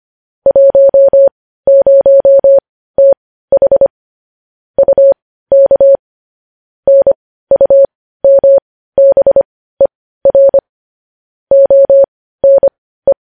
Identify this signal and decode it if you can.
Morse code, heard as a keyed tone, about 25 wpm, spaced as for 15 wpm, 550 Hz. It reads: 10TH UK NUMBER ONE
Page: news-headlines-in-morse-code-at-15-wpm